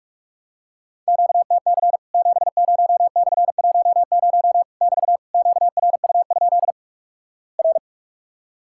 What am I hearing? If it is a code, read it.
Morse code, heard as a keyed tone, about 46 wpm, 700 Hz. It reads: QTC 70/10 <BT> QRU? R
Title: Worked All Europe DX Contest - QTC Exchange - 45wpm